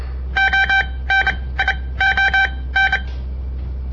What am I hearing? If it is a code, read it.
Morse code, heard as a keyed tone, about 29 wpm, spaced as for 19 wpm, 1550 Hz. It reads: ONION
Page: On the air Morse code practice from the Southern Montana Amatuer Radio Association; all code, no read back